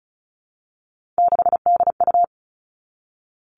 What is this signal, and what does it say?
Morse code, heard as a keyed tone, about 34 wpm, 700 Hz. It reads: 6DU
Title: All Cut Numbers - 3-Digit 35wpm